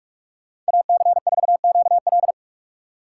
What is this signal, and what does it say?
Morse code, heard as a keyed tone, about 59 wpm, 700 Hz. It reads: AC4QL